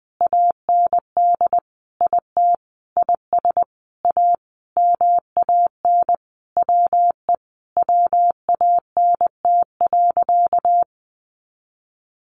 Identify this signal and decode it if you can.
Morse code, heard as a keyed tone, about 20 wpm, 700 Hz. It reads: AND IT IS A MAN WE WANT.
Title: Sentences from Top 200 Words 20wpm